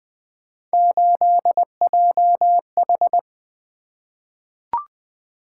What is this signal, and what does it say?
Morse code, heard as a keyed tone, about 20 wpm, 700 Hz. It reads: 8JH
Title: Three Letter-Number Combinations - 20wpm